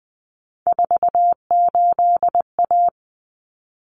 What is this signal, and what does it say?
Morse code, heard as a keyed tone, about 20 wpm, 700 Hz. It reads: 48A
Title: Common Cut Numbers - 3-Digit 20wpm